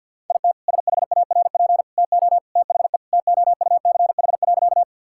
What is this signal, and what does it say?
Morse code, heard as a keyed tone, about 52 wpm, 700 Hz. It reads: IT SHARP TO THE TOUCH.
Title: Sentences from Top 1600 Words 50wpm